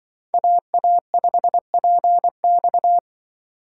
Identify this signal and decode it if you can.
Morse code, heard as a keyed tone, about 24 wpm, 700 Hz. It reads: AA5PX
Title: Speed Racing - USA 2x2 Call Signs 20wpm